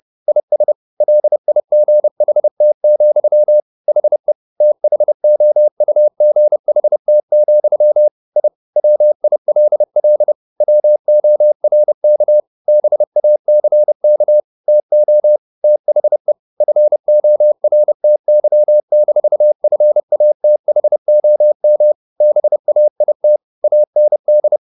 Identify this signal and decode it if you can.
Morse code, heard as a keyed tone, about 30 wpm, 600 Hz. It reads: IS LIGHT, HE THOUGHT, I WILL WORK BACK TO THE FORTY-FATHOM BAIT AND
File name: the old man and the sea / the old man and the sea 1 30wpm